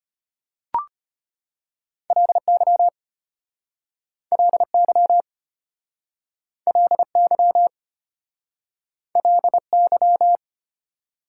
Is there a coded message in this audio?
Yes: Morse code from 2.1 to 10.4 s, heard as a keyed tone, about 32 wpm, 690 Hz. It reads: LY LY LY LY